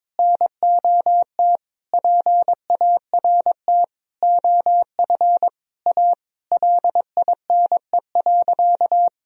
Morse code, heard as a keyed tone, about 22 wpm, 700 Hz. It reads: NOT PART OF A LINE.
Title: Sentences from Top 1000 Words - Review Part 1 - 22wpm